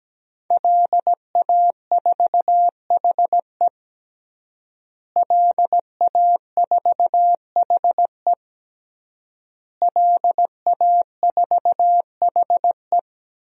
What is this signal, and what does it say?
Morse code, heard as a keyed tone, about 17 wpm, 700 Hz. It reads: LA4HE LA4HE LA4HE
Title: DXCC-100 Call Signs - 5 Characters - Repeated 3x - 17wpm